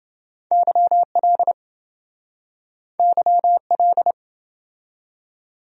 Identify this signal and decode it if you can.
Morse code, heard as a keyed tone, about 27 wpm, 700 Hz. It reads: YL YL